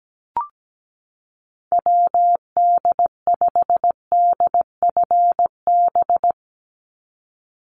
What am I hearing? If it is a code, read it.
Morse code, heard as a keyed tone, about 17 wpm, 700 Hz. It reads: WD5DFB
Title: USA Call Signs 17wpm